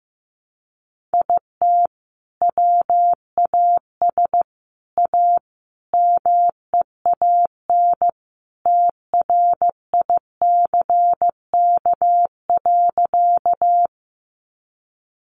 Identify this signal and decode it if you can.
Morse code, heard as a keyed tone, about 15 wpm, 700 Hz. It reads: IT WAS A MEAN TRICK.
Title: Sentences from Top 1500 Words 15wpm